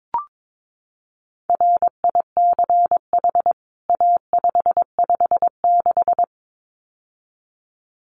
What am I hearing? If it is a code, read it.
Morse code, heard as a keyed tone, about 22 wpm, 700 Hz. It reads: RICH A556
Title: CWT Exchange 22wpm